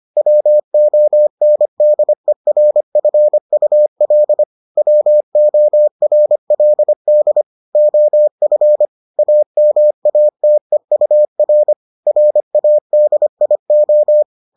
Morse code, heard as a keyed tone, about 25 wpm, 600 Hz. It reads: WONDERFUL WORLD OF AMATEUR RADIO